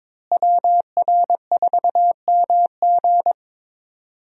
Morse code, heard as a keyed tone, about 22 wpm, 700 Hz. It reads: WR4MG
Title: USA Call Signs - 5 Characters - Repeated 3x - 22wpm